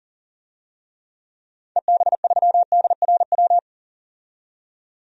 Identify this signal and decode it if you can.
Morse code, heard as a keyed tone, about 40 wpm, 700 Hz. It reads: EB3DRW
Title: DXCC-100 Call Signs - 6 Characters - 40wpm